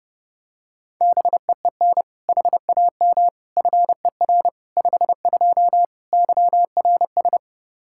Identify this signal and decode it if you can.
Morse code, heard as a keyed tone, about 32 wpm, 700 Hz. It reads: BEEN HAM FER 52 YRS
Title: Practice QSO - Exchange 3 - More Info - 30wpm